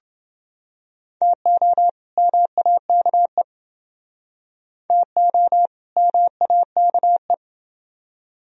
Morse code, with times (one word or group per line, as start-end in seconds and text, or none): 1.2-1.9 TO
2.2-3.4 MAKE
4.9-5.7 TO
6.0-7.3 MAKE